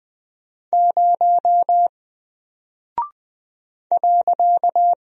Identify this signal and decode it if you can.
Morse code, heard as a keyed tone, about 20 wpm, 700 Hz. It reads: 0 .
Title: Single Character (Letter:Number:Punctuation:Prosign) Alt 20wpm